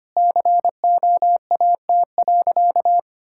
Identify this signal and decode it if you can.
Morse code, heard as a keyed tone, about 25 wpm, 700 Hz. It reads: COAT.
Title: Sentences from Top 1600 Words 25wpm